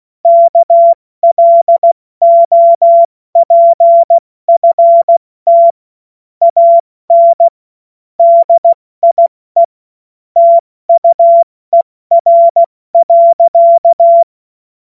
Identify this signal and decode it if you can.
Morse code, heard as a keyed tone, about 16 wpm, 680 Hz. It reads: KLOPFT AN DIE TUER.
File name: Witz-02-September-2025-70-F80